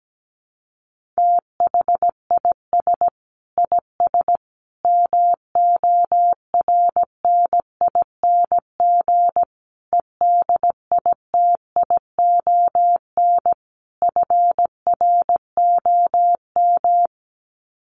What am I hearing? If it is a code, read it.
Morse code, heard as a keyed tone, about 17 wpm, 700 Hz. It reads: THIS IS MORNING EDITION FROM
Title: Sets of 5 Words Encore2 17wpm